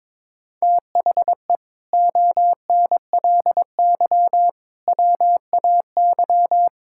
Morse code, heard as a keyed tone, about 22 wpm, 700 Hz. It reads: THE ONLY WAY
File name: practice / 22wpm / Sets of 8 Words - 22wpm